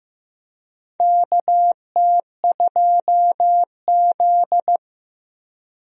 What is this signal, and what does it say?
Morse code, heard as a keyed tone, about 15 wpm, 700 Hz. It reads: KT2Z
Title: USA Call Signs - 4 Characters - 15wpm